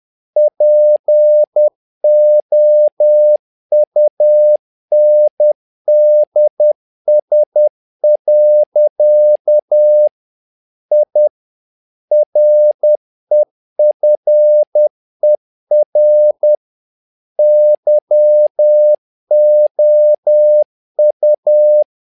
Morse code, heard as a keyed tone, about 10 wpm, 600 Hz. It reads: POUNDS. I REFER YOU
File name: learning center / tower trouble / tower trouble 10wpm